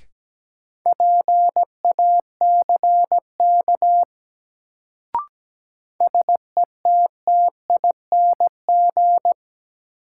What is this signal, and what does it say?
Morse code, heard as a keyed tone, about 17 wpm, 700 Hz. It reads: PACK SETTING
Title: Sentences from Top 1300 Words - New Word Drill - 17wpm